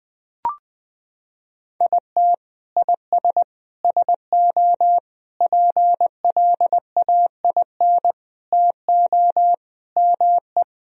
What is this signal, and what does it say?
Morse code, heard as a keyed tone, about 21 wpm, 700 Hz. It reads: IT IS SO PLAIN TO ME
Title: Sentences from Top 1000 Words - Review Part 2 - 20wpm